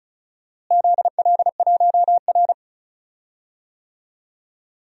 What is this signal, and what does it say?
Morse code, heard as a keyed tone, about 36 wpm, 700 Hz. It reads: ZL1R
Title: DXCC-100 Call Signs - 4 Characters - Repeated 3x - 35wpm